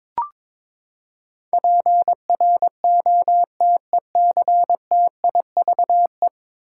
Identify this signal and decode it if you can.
Morse code, heard as a keyed tone, about 21 wpm, 700 Hz. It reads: PROTECTIVE
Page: morse-code-ninja-practice-22wpm